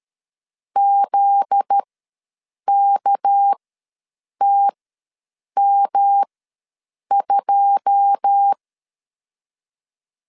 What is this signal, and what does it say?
Morse code, heard as a keyed tone, about 13 wpm, spaced as for 7 wpm, 800 Hz. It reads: ZKTM2
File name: LEZIONI / 10 SIGNS / 10C DOT ws 12wpm 3548